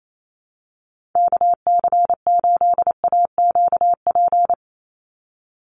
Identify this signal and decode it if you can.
Morse code, heard as a keyed tone, about 28 wpm, 700 Hz. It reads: KC8AQP
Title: USA Call Signs 28wpm